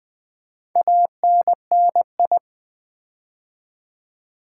Morse code, heard as a keyed tone, about 20 wpm, 700 Hz. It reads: ANNI
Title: Build A Word - 4 Letters At A Time - 11-Letter Words - 20wpm